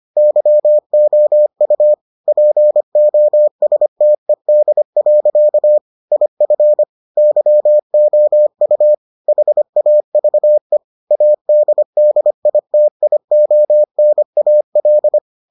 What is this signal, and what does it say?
Morse code, heard as a keyed tone, about 25 wpm, 600 Hz. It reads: YOU POSTED. IF YOU HAVE ADDITIONAL